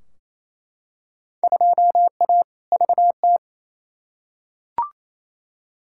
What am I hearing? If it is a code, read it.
Morse code, heard as a keyed tone, about 28 wpm, 700 Hz. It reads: 2A VT
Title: Field Day Exchanges Encore - 28wpm